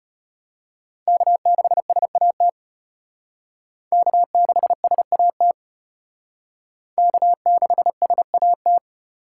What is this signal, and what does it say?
Morse code, heard as a keyed tone, about 34 wpm, 700 Hz. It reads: K6SAT K6SAT K6SAT